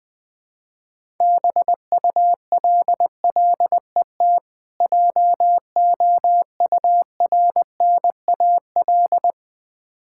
Morse code, heard as a keyed tone, about 20 wpm, 700 Hz. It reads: BULLET JOURNAL
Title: Difficult Words with Hints - 20wpm